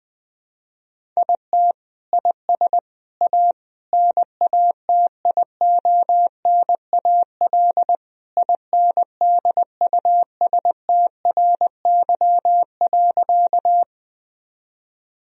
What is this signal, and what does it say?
Morse code, heard as a keyed tone, about 20 wpm, 700 Hz. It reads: IT IS A NATIONAL INDUSTRY.
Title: Sentences from Top 1000 Words - Review Part 4 - 20wpm